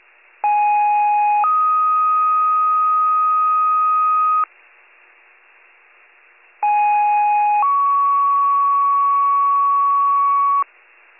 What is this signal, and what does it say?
Two-tone page, heard as a one-second tone, then a three-second tone, about 830 then 1250 Hz, about 830 then 1120 Hz. It is heard from 0.4 to 10.6 s.